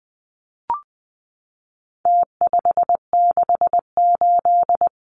Morse code, heard as a keyed tone, about 20 wpm, 700 Hz. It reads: T568
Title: Numbers - Contest Style - 4-Digit 20wpm